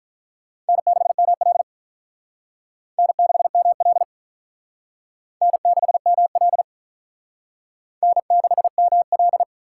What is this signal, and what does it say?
Morse code, heard as a keyed tone, about 45 wpm, 700 Hz. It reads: N6ML N6ML N6ML N6ML